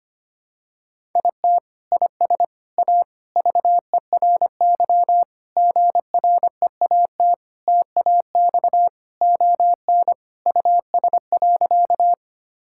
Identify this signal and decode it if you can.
Morse code, heard as a keyed tone, about 25 wpm, 700 Hz. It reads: IT IS A VERY GREAT TAX ON US.
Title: Sentences from Top 1000 Words - Review Part 5 - 25wpm